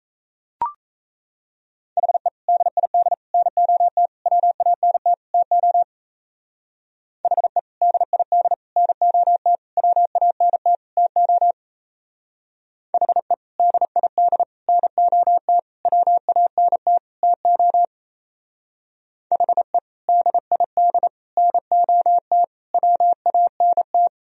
Morse code, heard as a keyed tone, about 35 wpm, 700 Hz. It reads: HE DID NOT WANT TO HE DID NOT WANT TO HE DID NOT WANT TO HE DID NOT WANT
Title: Speed-Racing - Sets of 5 Words 28wpm